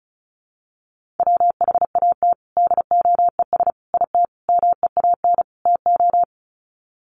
Morse code, heard as a keyed tone, about 34 wpm, 700 Hz. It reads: WHAT DOES IT MEAN TO
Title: Sets of 5 Words Encore2 35wpm